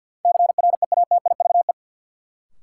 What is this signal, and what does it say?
Morse code, heard as a keyed tone, about 48 wpm, 700 Hz. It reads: CREATIVE
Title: Sentences from Top 1300 Words - New Word Drill Alt - 50wpm